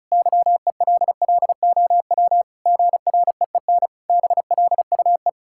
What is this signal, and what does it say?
Morse code, heard as a keyed tone, about 34 wpm, 700 Hz. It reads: YELLOW GREEN BLUE
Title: Pangram Challenge v2 - 35wpm